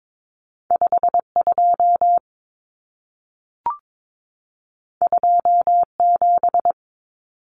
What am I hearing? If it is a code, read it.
Morse code, heard as a keyed tone, about 22 wpm, 700 Hz. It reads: 52 27
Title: Numbers - 2-Digit 22wpm